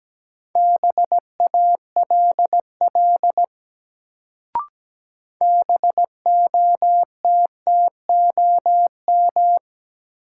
Morse code, heard as a keyed tone, about 17 wpm, 700 Hz. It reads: BALL BOTTOM
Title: Sentences from Top 900 Words - New Word Drill - 17wpm